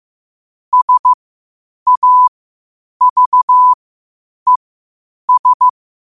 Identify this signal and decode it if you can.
Morse code, heard as a keyed tone, about 15 wpm, spaced as for 8 wpm, 1000 Hz. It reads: SAVES
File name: qotdfiles / 8wpm